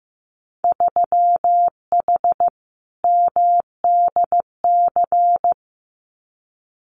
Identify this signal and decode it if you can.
Morse code, heard as a keyed tone, about 15 wpm, 700 Hz. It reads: 3H MDC
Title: Winter Field Day Exchanges Encore 15wpm